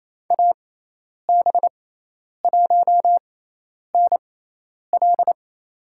Morse code, heard as a keyed tone, about 28 wpm, spaced as for 10 wpm, 700 Hz. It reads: AB1NL
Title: USA Call Signs - 6x Character Spacing - 28wpm